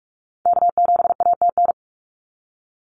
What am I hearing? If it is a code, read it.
Morse code, heard as a keyed tone, about 46 wpm, 700 Hz. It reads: K7ATN
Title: NA SOTA Activator Callsigns 45wpm